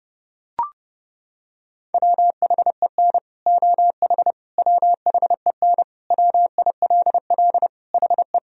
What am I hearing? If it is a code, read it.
Morse code, heard as a keyed tone, about 30 wpm, 700 Hz. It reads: WHEN OH WHEN WILL HE
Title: Sentences from Top 1000 Words - Review Part 2 - 30wpm